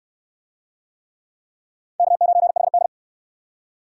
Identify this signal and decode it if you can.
Morse code, heard as a keyed tone, about 68 wpm, 700 Hz. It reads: K9HD